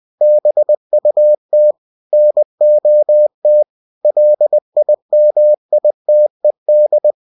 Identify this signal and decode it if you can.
Morse code, heard as a keyed tone, about 20 wpm, 600 Hz. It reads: BUT NOT LIMITED
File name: cw-books / winnie-the-pooh / winnie-20wpm-page-0119